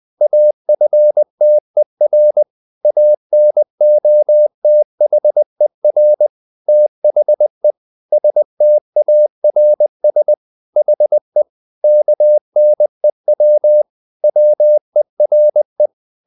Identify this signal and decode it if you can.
Morse code, heard as a keyed tone, about 20 wpm, 600 Hz. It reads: AFTER ANOTHER THE STARS HE KNEW WERE
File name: the old man and the sea part 2 20wpm